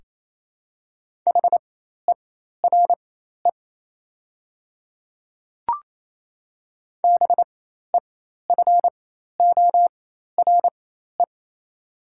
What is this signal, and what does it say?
Morse code, heard as a keyed tone, about 28 wpm, spaced as for 13 wpm, 700 Hz. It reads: HERE BEFORE
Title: 4x Character Spacing - Top 300 Words - 28wpm